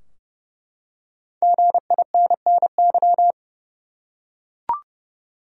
Morse code, heard as a keyed tone, about 30 wpm, 700 Hz. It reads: GINNY
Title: USA Common Names - 30wpm